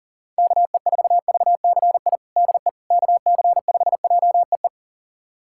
Morse code, heard as a keyed tone, about 37 wpm, 700 Hz. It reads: KE4VCI DE KC5JEE
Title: USA Callsign DE Callsign - 6 Characters - 40wpm